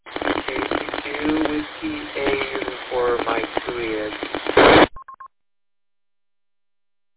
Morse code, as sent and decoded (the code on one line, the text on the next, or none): none